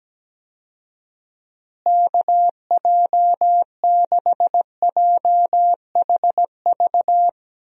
Morse code, heard as a keyed tone, about 17 wpm, 700 Hz. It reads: KJ6JHV
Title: USA Call Signs - 6 Characters - Repeated 2x - 17wpm